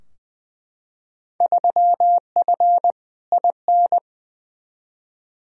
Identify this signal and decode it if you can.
Morse code, heard as a keyed tone, about 20 wpm, 700 Hz. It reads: 3F IN
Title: Field Day Exchanges - 20wpm